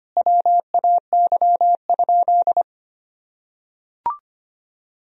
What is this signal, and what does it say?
Morse code, heard as a keyed tone, about 25 wpm, 700 Hz. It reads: WAY?
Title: Sentences from Top 1000 Words - Review Part 2 - 25wpm